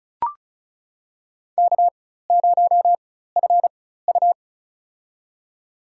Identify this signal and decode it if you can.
Morse code, heard as a keyed tone, about 36 wpm, spaced as for 16 wpm, 700 Hz. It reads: K0FU
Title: USA Call Signs - 4 Characters - 4x Character Spacing - 35wpm